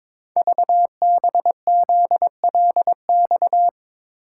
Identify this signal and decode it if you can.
Morse code, heard as a keyed tone, about 22 wpm, 700 Hz. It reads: VBZLX
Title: Code Groups - 5 Letters - 22wpm